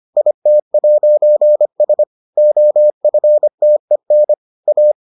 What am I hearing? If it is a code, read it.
Morse code, heard as a keyed tone, about 25 wpm, 600 Hz. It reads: IT'S OFTEN A